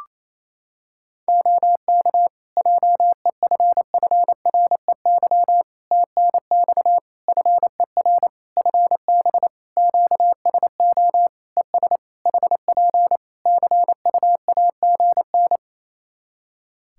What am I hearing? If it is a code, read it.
Morse code, heard as a keyed tone, about 28 wpm, 700 Hz. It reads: OK JEFFREY TNX FER FB QSO ES HP CUAGN
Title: Practice QSO - Exchange 4 - Ending - 28wpm